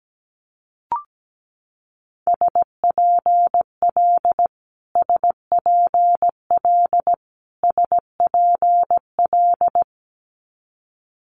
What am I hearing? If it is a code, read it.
Morse code, heard as a keyed tone, about 17 wpm, 700 Hz. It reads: SPL SPL SPL
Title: Multi-letter Phonemes Repeated 3x 17wpm